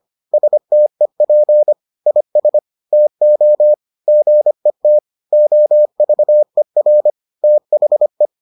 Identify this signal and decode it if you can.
Morse code, heard as a keyed tone, about 25 wpm, 600 Hz. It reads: STEP IS TO GET OVER THE